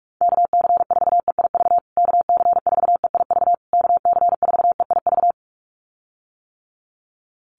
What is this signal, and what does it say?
Morse code, heard as a keyed tone, about 44 wpm, 690 Hz. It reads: KC4EIV KC4EIV KC4EIV
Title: USA Call Signs Repeated 3x 45wpm